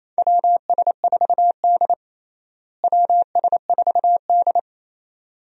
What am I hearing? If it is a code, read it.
Morse code, heard as a keyed tone, about 28 wpm, 700 Hz. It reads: WS4D WS4D
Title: USA Call Signs - 4 Characters - Repeated 2x - 28wpm